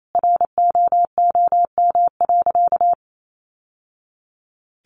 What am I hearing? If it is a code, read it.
Morse code, heard as a keyed tone, about 28 wpm, 700 Hz. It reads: ROOM.
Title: Sentences from Top 1100 Words 28wpm